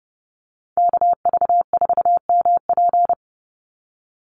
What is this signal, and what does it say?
Morse code, heard as a keyed tone, about 30 wpm, 700 Hz. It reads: KV4MP
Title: USA Call Signs 30wpm